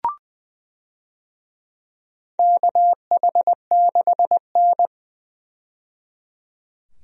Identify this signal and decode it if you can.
Morse code, heard as a keyed tone, about 20 wpm, 700 Hz. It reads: KH6N
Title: DXCC-100 Call Signs - 4 Characters - 20wpm